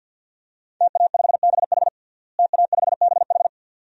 Morse code, heard as a keyed tone, about 52 wpm, 700 Hz. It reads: TA5BH TA5BH
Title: DXCC-100 Call Signs - 5 Characters - Repeated 2x - 50wpm